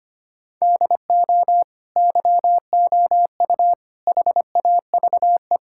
Morse code, heard as a keyed tone, about 25 wpm, 700 Hz. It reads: DO YOU HAVE
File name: practice / 25wpm / Sentences with 3-Word Phrases - 25wpm